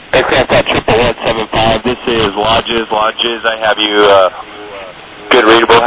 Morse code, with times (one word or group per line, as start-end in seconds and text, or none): none